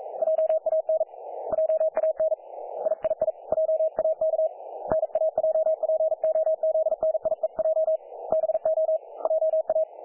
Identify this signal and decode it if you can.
Morse code, heard as a keyed tone, about 40 wpm, 630 Hz. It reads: JAN JAN III JAK NAJPOZNIEJ BO JA
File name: dlazaawansowanych / nagraniazpasma / websdr recording 3564.7kHz